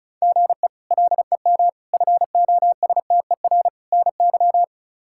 Morse code, heard as a keyed tone, about 36 wpm, 700 Hz. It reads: GE LEM FOSTER NY
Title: K1USN SST QSO Practice - 35wpm